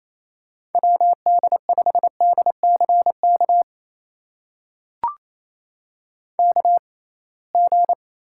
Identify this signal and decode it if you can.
Morse code, heard as a keyed tone, about 28 wpm, spaced as for 21 wpm, 700 Hz. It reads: WD5DCK KG
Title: USA Call Signs - 6 Characters - 6x Character Spacing - 28wpm